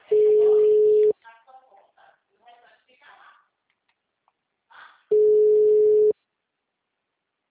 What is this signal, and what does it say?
Ringback tone (Europe, 425 Hz), with speech.
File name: out- -2020- - - .219